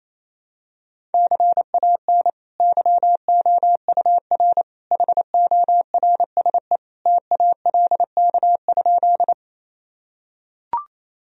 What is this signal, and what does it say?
Morse code, heard as a keyed tone, about 28 wpm, 700 Hz. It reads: CAN YOUR HORSE TALK?
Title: Sentences from Top 1000 Words - Review Part 1 - 28wpm